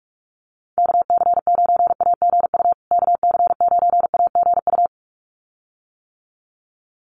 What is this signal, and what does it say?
Morse code, heard as a keyed tone, about 44 wpm, 700 Hz. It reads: KC9AGU KC9AGU
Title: USA Call Signs Repeated 2x 45wpm